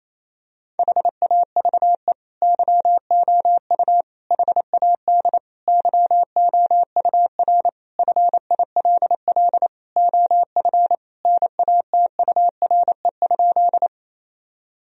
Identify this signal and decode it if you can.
Morse code, heard as a keyed tone, about 28 wpm, 700 Hz. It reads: HAVE YOU HAD YOUR FILL OF NATURE?
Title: Sentences from Top 1000 Words - Review Part 3 - 28wpm